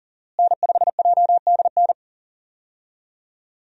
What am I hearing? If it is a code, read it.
Morse code, heard as a keyed tone, about 40 wpm, 700 Hz. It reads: N5JDN
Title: USA Call Signs - 5 Characters - Repeated 3x - 40wpm